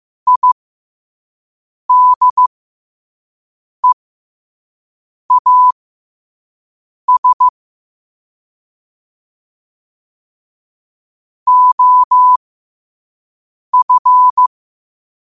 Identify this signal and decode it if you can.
Morse code, heard as a keyed tone, about 15 wpm, spaced as for 5 wpm, 1000 Hz. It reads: IDEAS OF
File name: qotdfiles / 5wpm